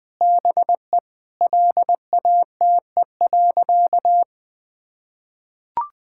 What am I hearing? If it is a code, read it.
Morse code, heard as a keyed tone, about 20 wpm, 700 Hz. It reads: BE LATE.
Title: Sentences with 5-Word Phrases - Spoken Beginning - 20wpm